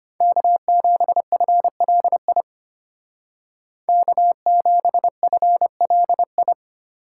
Morse code, heard as a keyed tone, about 33 wpm, 700 Hz. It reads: K7FLI K7FLI